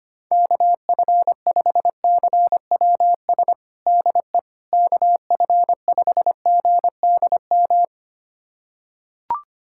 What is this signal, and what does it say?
Morse code, heard as a keyed tone, about 25 wpm, 700 Hz. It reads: KF5CWS DE KF5GDM
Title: USA Callsign DE Callsign - 6 Characters - 25wpm